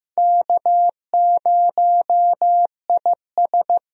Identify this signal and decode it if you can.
Morse code, heard as a keyed tone, about 15 wpm, 700 Hz. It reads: K0IS
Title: USA Call Signs - 4 Characters - 15wpm